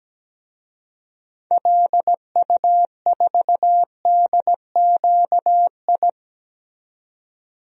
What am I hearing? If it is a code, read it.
Morse code, heard as a keyed tone, about 17 wpm, 700 Hz. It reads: LU4DQI
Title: DXCC-100 Call Signs - 6 Characters - 17wpm